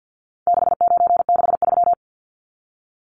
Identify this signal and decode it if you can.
Morse code, heard as a keyed tone, about 50 wpm, 700 Hz. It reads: 6963
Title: Numbers - 4-Digit 50wpm